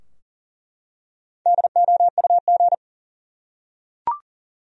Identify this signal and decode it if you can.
Morse code, heard as a keyed tone, about 40 wpm, 700 Hz. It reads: DOUG
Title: USA Common Names - Max 4 Letters - 40wpm